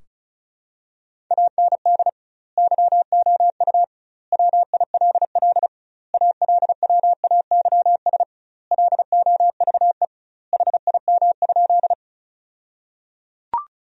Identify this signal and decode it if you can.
Morse code, heard as a keyed tone, about 36 wpm, 700 Hz. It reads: AND YOU WILL ALWAYS LOVE HIM?
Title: 2x Word-Spacing - Sentences from Top 400 Words - 35wpm